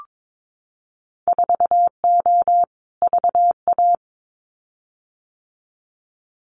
Morse code, heard as a keyed tone, about 22 wpm, 700 Hz. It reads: 4O VA
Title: Winter Field Day Exchanges Encore 22wpm